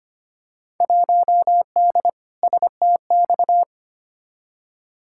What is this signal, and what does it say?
Morse code, heard as a keyed tone, about 25 wpm, 700 Hz. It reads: 1D STX
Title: Field Day Exchanges - 25wpm